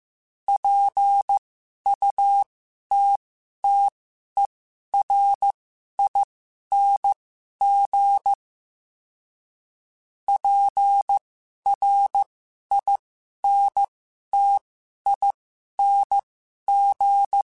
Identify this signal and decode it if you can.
Morse code, heard as a keyed tone, about 15 wpm, spaced as for 11 wpm, 800 Hz. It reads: PUTTERING PRINTING
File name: Special INGs-mp3s / Special INGs-mp3s / Sngl Ing @10